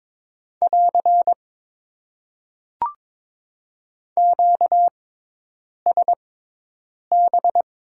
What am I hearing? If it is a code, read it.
Morse code, heard as a keyed tone, about 22 wpm, spaced as for 8 wpm, 700 Hz. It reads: <AR> QSB
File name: Top 90 QSO Elements - 6x Character Spacing - 22wpm